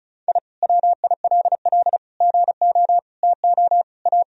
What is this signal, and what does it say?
Morse code, heard as a keyed tone, about 36 wpm, 700 Hz. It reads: I WILL GO TO A
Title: Sentences from Top 1000 Words - Review Part 4 - 35wpm